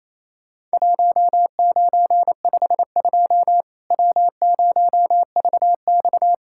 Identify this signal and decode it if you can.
Morse code, heard as a keyed tone, about 28 wpm, 700 Hz. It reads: 1952 W0VX
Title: Worked All Europe DX Contest - QTC Exchange - 28wpm